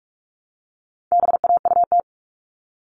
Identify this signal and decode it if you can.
Morse code, heard as a keyed tone, about 44 wpm, 700 Hz. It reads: BAUT
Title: All Cut Numbers - 4-Digit 45wpm